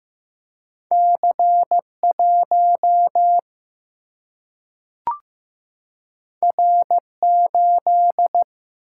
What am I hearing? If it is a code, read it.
Morse code, heard as a keyed tone, about 15 wpm, 700 Hz. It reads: C1 R8
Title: Single-Letter Single-Number 15wpm